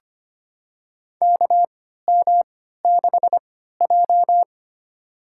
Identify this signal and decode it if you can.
Morse code, heard as a keyed tone, about 25 wpm, spaced as for 14 wpm, 700 Hz. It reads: KM6J
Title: USA Call Signs - 3x Character Spacing - 25wpm